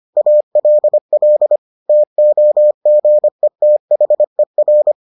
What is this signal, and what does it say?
Morse code, heard as a keyed tone, about 25 wpm, 600 Hz. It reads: ALL TOGETHER